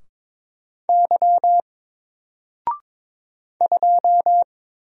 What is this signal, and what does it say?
Morse code, heard as a keyed tone, about 22 wpm, 700 Hz. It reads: Y 2
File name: Single-Letter-Number-22wpm